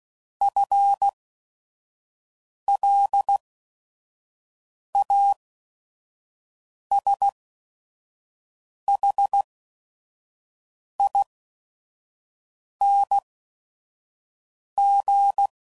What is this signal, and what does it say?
Morse code, heard as a keyed tone, about 16 wpm, spaced as for 5 wpm, 800 Hz. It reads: FLASHING
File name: Prefixes / FLs & THs / FL Runs